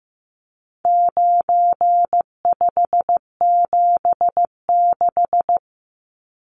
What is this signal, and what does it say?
Morse code, heard as a keyed tone, about 15 wpm, 700 Hz. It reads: 9576
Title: Numbers - 4-Digit 15wpm